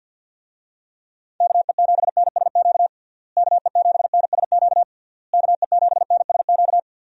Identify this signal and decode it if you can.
Morse code, heard as a keyed tone, about 50 wpm, 700 Hz. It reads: KE7NSQ KE7NSQ KE7NSQ
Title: USA Call Signs - 6 Characters - Repeated 3x - 50wpm